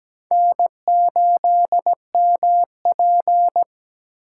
Morse code, heard as a keyed tone, about 17 wpm, 700 Hz. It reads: N8MP
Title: USA Call Signs - 4 Characters - 8x Character Spacing - 17wpm